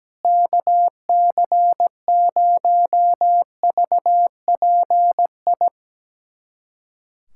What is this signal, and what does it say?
Morse code, heard as a keyed tone, about 17 wpm, 700 Hz. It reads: KC0VPI
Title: USA Callsign DE Callsign 17wpm